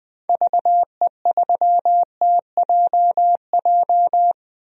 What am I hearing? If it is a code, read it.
Morse code, heard as a keyed tone, about 20 wpm, 700 Hz. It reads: VE3TJJ
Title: DXCC-100 Call Signs - 6 Characters - Repeated 3x - 20wpm